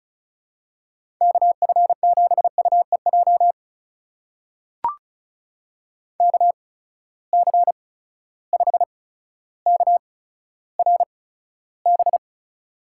Morse code, heard as a keyed tone, about 36 wpm, spaced as for 15 wpm, 700 Hz. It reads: KF7UEJ KC5KRB
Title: USA Call Signs - 6 Characters - 8x Character Spacing - 35wpm